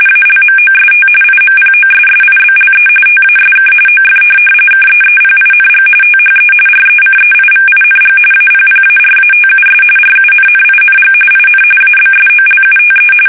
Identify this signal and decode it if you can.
RTTY, about 45.45 baud. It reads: ALMOST IMPOSSIBLE TASK OF KEEPING THE RECEIVING APPARATUS RUNNING IN ABSOLUTE
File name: fsk850